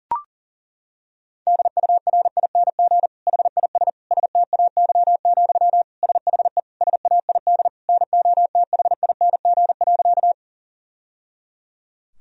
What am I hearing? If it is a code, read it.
Morse code, heard as a keyed tone, about 40 wpm, 700 Hz. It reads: DURING HIS STAY, SHE SAID NOTHING.
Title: Sentences from Top 1000 Words - Review Part 2 - 40wpm